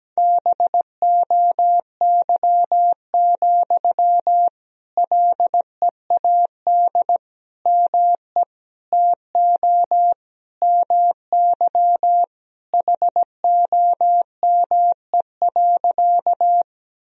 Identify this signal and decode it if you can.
Morse code, heard as a keyed tone, about 17 wpm, 700 Hz. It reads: BOY, LEAD ME TO MY HOME.
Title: Sentences from Top 1000 Words - Review Part 2 - 17wpm